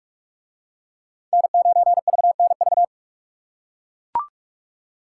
Morse code, heard as a keyed tone, about 40 wpm, 700 Hz. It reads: N9VNV